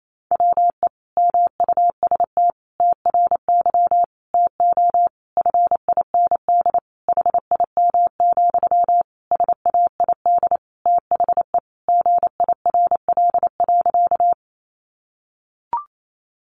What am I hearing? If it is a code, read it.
Morse code, heard as a keyed tone, about 28 wpm, 700 Hz. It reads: WE MUST TRY TO FIND HIM, SAID THE GIRL.
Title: Sentences from Top 400 Words 28wpm